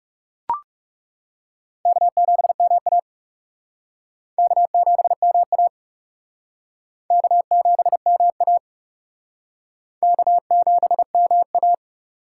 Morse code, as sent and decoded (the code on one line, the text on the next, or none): -.- --... -- .- / -.- --... -- .- / -.- --... -- .- / -.- --... -- .-
K7MA K7MA K7MA K7MA